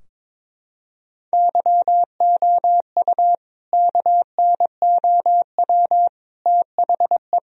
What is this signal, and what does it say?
Morse code, heard as a keyed tone, about 21 wpm, 700 Hz. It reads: YOU KNOW THE
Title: Sentences from Top 1000 Words - Review Part 2 - 22wpm